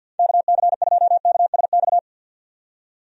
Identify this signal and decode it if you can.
Morse code, heard as a keyed tone, about 52 wpm, 700 Hz. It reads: KC2KSX
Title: USA Call Signs - 6 Characters - 50wpm